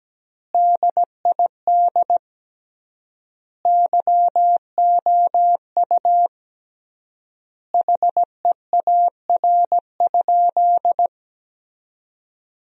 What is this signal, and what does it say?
Morse code, heard as a keyed tone, about 17 wpm, 700 Hz. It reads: DID YOU HEAR?
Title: 3x Word-Spacing - Sentences from Top 300 Words - 17wpm